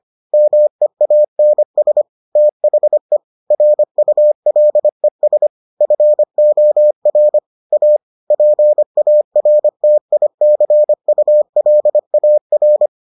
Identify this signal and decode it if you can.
Morse code, heard as a keyed tone, about 25 wpm, 600 Hz. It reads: MEANS THE RULES FOR A PARTICULAR